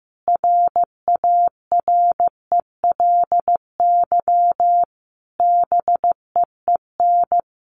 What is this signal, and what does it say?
Morse code, heard as a keyed tone, about 15 wpm, 700 Hz. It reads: RARELY BEEN
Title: Sentences from Top 1400 Words 15wpm